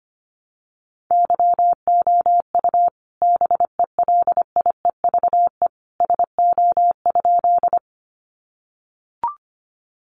Morse code, heard as a keyed tone, about 25 wpm, 700 Hz. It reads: YOU BELIEVE SO?
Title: Sentences from Top 1000 Words 25wpm